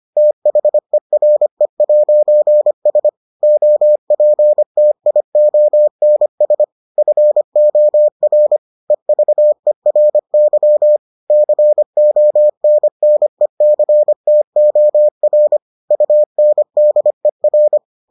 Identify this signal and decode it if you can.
Morse code, heard as a keyed tone, about 25 wpm, 600 Hz. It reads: THERE'S OPTIONS FOR EVERY CONNECTOR UNDER